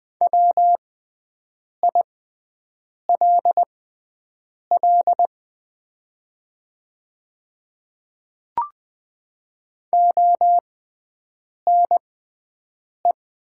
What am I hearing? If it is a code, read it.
Morse code, heard as a keyed tone, about 20 wpm, spaced as for 7 wpm, 700 Hz. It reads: WILL ONE
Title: 6x Character Spacing - Top 200 Words - 20wpm